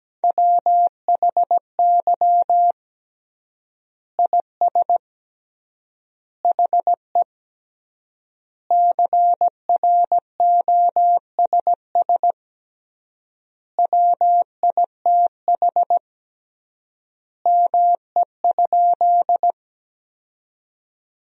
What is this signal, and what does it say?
Morse code, heard as a keyed tone, about 17 wpm, 700 Hz. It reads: WHY IS HE CROSS WITH ME?
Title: 3x Word-Spacing - Sentences from Top 300 Words - 17wpm